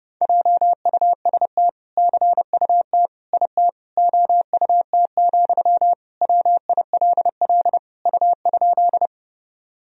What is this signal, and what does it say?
Morse code, heard as a keyed tone, about 30 wpm, 700 Hz. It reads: JUST CUT IT OUT, WILL U?
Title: Sentences from Top 300 Words and Abbreviations 30wpm